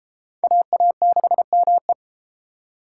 Morse code, heard as a keyed tone, about 33 wpm, 700 Hz. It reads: AA6ME